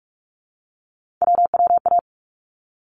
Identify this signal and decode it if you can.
Morse code, heard as a keyed tone, about 44 wpm, 700 Hz. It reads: WWA
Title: ARRL Sections 45wpm